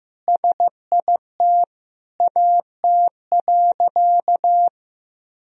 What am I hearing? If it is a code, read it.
Morse code, heard as a keyed tone, about 16 wpm, 700 Hz. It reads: SIT AT.
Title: Sentences from Top 1000 Words - Review Part 2 - 15wpm